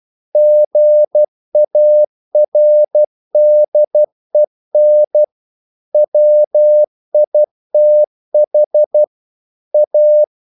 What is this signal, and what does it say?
Morse code, heard as a keyed tone, about 12 wpm, 600 Hz. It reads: GARDEN WITH A
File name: mp3 files / the gardener - 12wpm